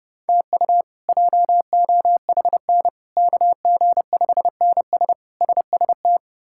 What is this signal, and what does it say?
Morse code, heard as a keyed tone, about 30 wpm, 700 Hz. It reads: TU JOHN KG5NS SST
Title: K1USN SST QSO Practice - 30wpm